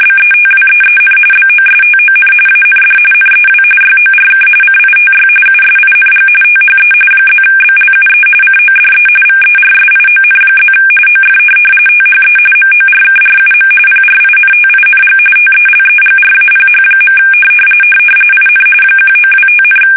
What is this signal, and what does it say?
RTTY, about 45.45 baud. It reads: BETWEEN BOSTON AND NEW YORK CITY USING THE +BLUE CODE VERSION+ OF THE MORKRUM PRINTING TELEGRAPH. IN 1925,
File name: fsk850 / teletype-corp